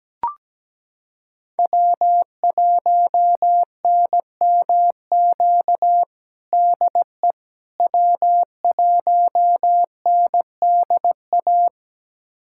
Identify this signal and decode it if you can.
Morse code, heard as a keyed tone, about 17 wpm, 700 Hz. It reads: W1NMQ DE W1NDA
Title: USA Callsign DE Callsign 17wpm